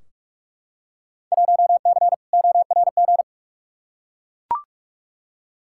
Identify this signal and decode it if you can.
Morse code, heard as a keyed tone, about 43 wpm, 700 Hz. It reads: 1C ORG
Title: Field Day Exchanges - 45wpm